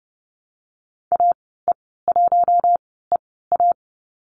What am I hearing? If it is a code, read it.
Morse code, heard as a keyed tone, about 30 wpm, spaced as for 17 wpm, 700 Hz. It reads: AE1EA
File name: Lesson 14g - Number 2 - Call Signs 30wpm